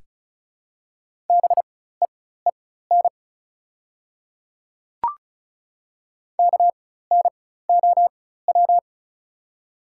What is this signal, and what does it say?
Morse code, heard as a keyed tone, about 36 wpm, spaced as for 16 wpm, 700 Hz. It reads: BEEN KNOW
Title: 4x Character Spacing - Top 100 Words - 35wpm